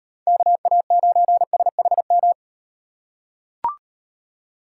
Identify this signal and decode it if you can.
Morse code, heard as a keyed tone, about 38 wpm, 700 Hz. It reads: KA9SHM